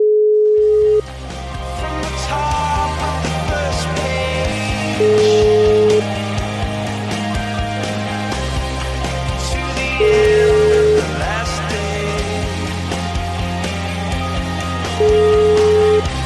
Ringback tone (Europe, 425 Hz), with noise.